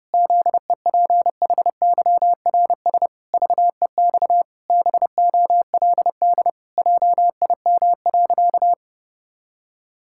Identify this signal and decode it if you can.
Morse code, heard as a keyed tone, about 30 wpm, 700 Hz. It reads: ZEPHYRS VEX BOLD JIM.
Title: Pangram Challenge v2 - 30wpm